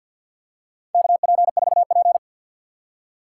Morse code, heard as a keyed tone, about 50 wpm, 700 Hz. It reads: KP4P
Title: USA Call Signs - 2x Character Spacing - 50wpm